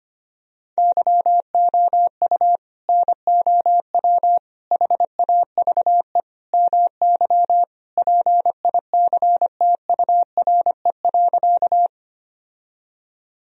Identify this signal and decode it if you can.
Morse code, heard as a keyed tone, about 25 wpm, 700 Hz. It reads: YOU NOW HAVE MY PICTURE.
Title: Sentences from Top 1000 Words - Review Part 1 - 25wpm